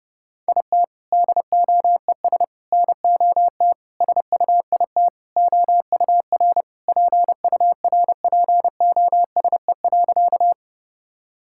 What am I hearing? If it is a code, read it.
Morse code, heard as a keyed tone, about 32 wpm, 700 Hz. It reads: IT DOES NOT SUIT OUR PURPOSE.
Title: Sentences from Top 1600 Words 30wpm